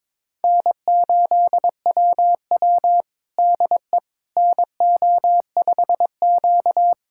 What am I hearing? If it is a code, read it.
Morse code, heard as a keyed tone, about 22 wpm, 700 Hz. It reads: N8WW DE NO5Q
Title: USA Callsign DE Callsign - 4 Characters - 22wpm